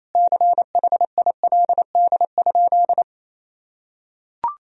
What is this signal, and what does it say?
Morse code, heard as a keyed tone, about 28 wpm, 700 Hz. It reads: CHILD?
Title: Sentences from Top 1000 Words - Review Part 2 - 28wpm